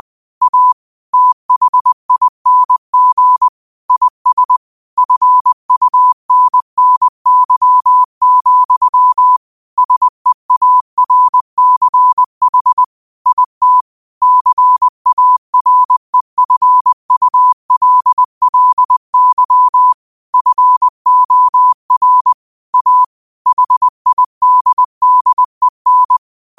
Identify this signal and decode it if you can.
Morse code, heard as a keyed tone, about 20 wpm, 1000 Hz. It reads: A THING IS FUNNY, SEARCH IT CAREFULLY FOR A HIDDEN
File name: qotdfiles / 20wpm